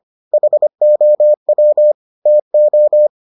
Morse code, heard as a keyed tone, about 25 wpm, 600 Hz. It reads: HOW TO